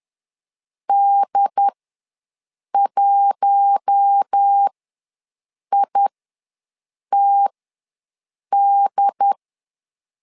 Morse code, heard as a keyed tone, about 10 wpm, spaced as for 6 wpm, 800 Hz. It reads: D1ITD
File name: LEZIONI / 4 Q1 ws 10wpm 2538